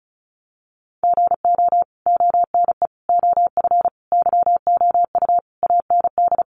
Morse code, heard as a keyed tone, about 34 wpm, 700 Hz. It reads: GO ONE OF YOU AND
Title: Sentences from Top 1100 Words 35wpm